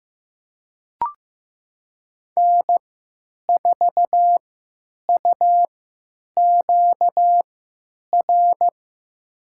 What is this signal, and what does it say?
Morse code, heard as a keyed tone, about 15 wpm, spaced as for 9 wpm, 700 Hz. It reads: N4UQR
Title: USA Call Signs - 5 Characters - 3x Character Spacing - 15wpm